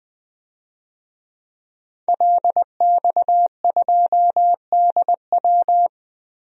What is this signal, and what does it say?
Morse code, heard as a keyed tone, about 20 wpm, 700 Hz. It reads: LX2DW
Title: DXCC-100 Call Signs - 5 Characters - Repeated 2x - 20wpm